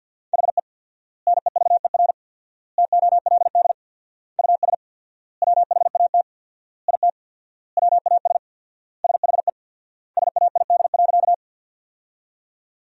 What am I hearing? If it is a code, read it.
Morse code, heard as a keyed tone, about 52 wpm, 700 Hz. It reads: HE NEVER TOLD US WHAT IT WAS SHE SAID.
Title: 4x Word-Spacing - Sentences from Top 400 Words - 50wpm